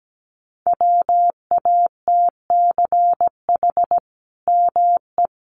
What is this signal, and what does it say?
Morse code, heard as a keyed tone, about 17 wpm, 700 Hz. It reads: WATCH ME
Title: Sentences from Top 1300 Words 17wpm